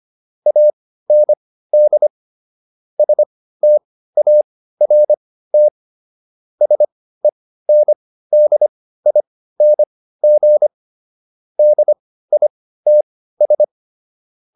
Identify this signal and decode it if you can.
Morse code, heard as a keyed tone, about 25 wpm, spaced as for 15 wpm, 600 Hz. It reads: AND START SENDING DITS